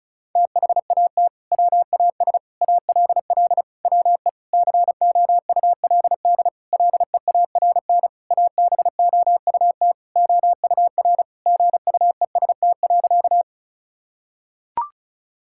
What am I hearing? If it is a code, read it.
Morse code, heard as a keyed tone, about 34 wpm, 700 Hz. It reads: THAT WAS ALL WE COULD LEARN ABOUT OUR GUEST.
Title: Sentences from Top 1600 Words 35wpm